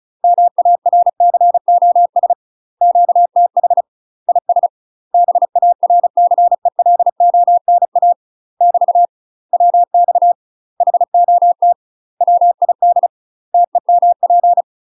Morse code, heard as a keyed tone, about 35 wpm, 700 Hz. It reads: MARCOS QTH IS BARCELONA <BT> WX HOT WID TEMP
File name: UK MASTER QSOs TRAINING / 35WPM / UK MASTER QSOs 0003